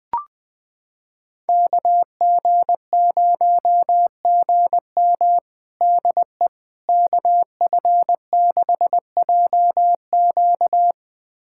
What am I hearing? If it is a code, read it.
Morse code, heard as a keyed tone, about 21 wpm, 700 Hz. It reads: KG0GM DE KF6JQ
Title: USA Callsign DE Callsign - 5 Characters - 20wpm